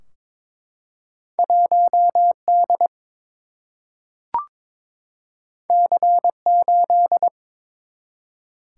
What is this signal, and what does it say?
Morse code, heard as a keyed tone, about 21 wpm, 700 Hz. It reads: 1D C8
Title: Single-Letter Single-Number 22wpm